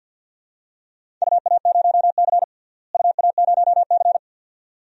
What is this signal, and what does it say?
Morse code, heard as a keyed tone, about 50 wpm, 700 Hz. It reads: UA0C UA0C
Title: DXCC-100 Call Signs - 4 Characters - Repeated 2x - 50wpm